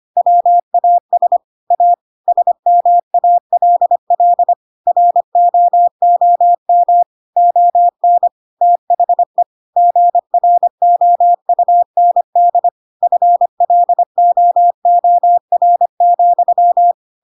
Morse code, heard as a keyed tone, about 25 wpm, 700 Hz. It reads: WAS A SMALL ROOM ON THE GROUND FLOOR,
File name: Pinocchio / Pinocchio ch3 25 25 ik1hge